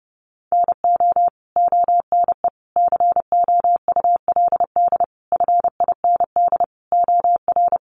Morse code, heard as a keyed tone, about 30 wpm, 700 Hz. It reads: NO ONE COULD FIND OR
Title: Sentences from Top 1100 Words 30wpm